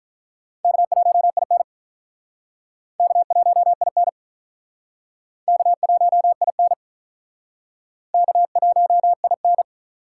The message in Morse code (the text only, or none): K1IN K1IN K1IN K1IN